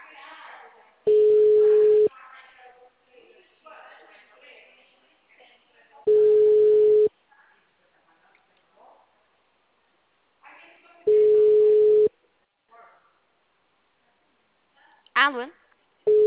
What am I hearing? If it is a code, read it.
Ringback tone (Europe, 425 Hz), with speech.